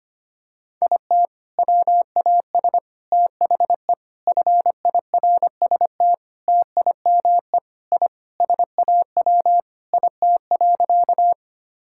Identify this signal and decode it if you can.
Morse code, heard as a keyed tone, about 25 wpm, 700 Hz. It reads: IT WAS THE FIRST TIME I SAW IT.
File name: practice / 25wpm / Sentences with 3-Word Phrases - 25wpm